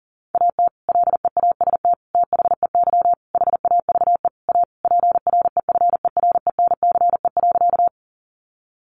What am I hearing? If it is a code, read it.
Morse code, heard as a keyed tone, about 40 wpm, 700 Hz. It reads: AT LEAST THEY HAVE A PREFERENCE.
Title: Sentences from Top 1500 Words 40wpm